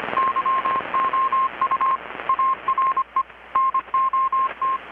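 Morse code, heard as a keyed tone, about 25 wpm, 1060 Hz, with radio noise. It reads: YOU ARE NOT
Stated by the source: Morse from KPH's Night of Nights XX maritime radiotelegraphy event, received in Chiapas, Mexico about 3,800 km away on a Kenwood R-600 with a grounded 10 m long-wire antenna.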